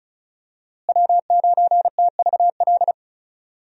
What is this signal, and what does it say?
Morse code, heard as a keyed tone, about 34 wpm, 700 Hz. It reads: W9TVL
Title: USA Call Signs - 5 Characters - Repeated 2x - 35wpm